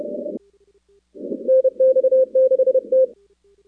Morse code, heard as a keyed tone, about 31 wpm, 520 Hz. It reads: NX6T
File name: contests / 2019-cw-wpx-kt5j-k5pi- -2330 0-4